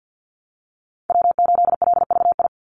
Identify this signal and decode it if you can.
Morse code, heard as a keyed tone, about 50 wpm, 700 Hz. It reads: W8LVI
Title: Call Signs Repeated 2x - 50wpm